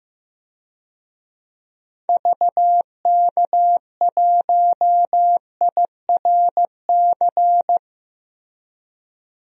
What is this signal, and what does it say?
Morse code, heard as a keyed tone, about 15 wpm, 700 Hz. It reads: VK1IRC
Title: DXCC-100 Call Signs - 6 Characters - Repeated 2x - 15wpm